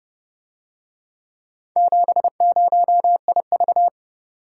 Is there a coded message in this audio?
Yes: Morse code from 1.8 to 3.9 s, heard as a keyed tone, about 29 wpm, 700 Hz. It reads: 70IV